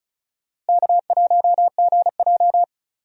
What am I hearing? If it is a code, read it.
Morse code, heard as a keyed tone, about 36 wpm, 700 Hz. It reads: K1GJ
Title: USA Call Signs - 4 Characters - Repeated 2x - 35wpm